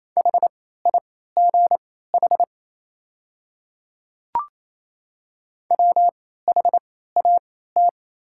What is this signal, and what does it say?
Morse code, heard as a keyed tone, about 27 wpm, spaced as for 16 wpm, 700 Hz. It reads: HIGH WHAT
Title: 3x Character Spacing - Top 200 Words - 28wpm